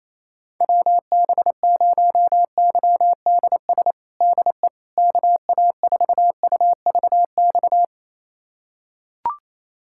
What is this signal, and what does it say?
Morse code, heard as a keyed tone, about 28 wpm, 700 Hz. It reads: WB0YDS DE KA4UVX
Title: USA Callsign DE Callsign - 6 Characters - 28wpm